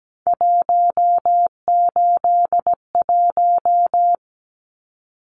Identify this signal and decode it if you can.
Morse code, heard as a keyed tone, about 17 wpm, 700 Hz. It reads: 181
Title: Numbers - 3-Digit 17wpm